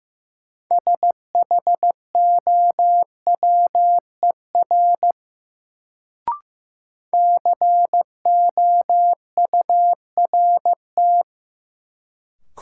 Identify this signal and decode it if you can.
Morse code, heard as a keyed tone, about 15 wpm, 700 Hz. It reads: SHOWER COURT
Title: Sentences from Top 1600 Words - New Word Drill Alt - 15wpm